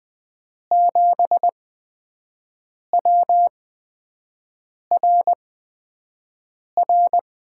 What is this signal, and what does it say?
Morse code, heard as a keyed tone, about 20 wpm, spaced as for 5 wpm, 700 Hz. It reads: 7WRR
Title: USA Call Signs - 5 Characters - 8x Character Spacing - 20wpm